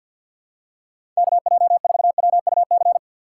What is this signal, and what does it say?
Morse code, heard as a keyed tone, about 52 wpm, 700 Hz. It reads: KJ4WUC
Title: USA Call Signs - 6 Characters - 8x Character Spacing - 50wpm